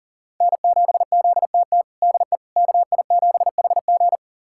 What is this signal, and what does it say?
Morse code, heard as a keyed tone, about 40 wpm, 700 Hz. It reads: N7ZTT DE KI7HG
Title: USA Callsign DE Callsign - 5 Characters - 40wpm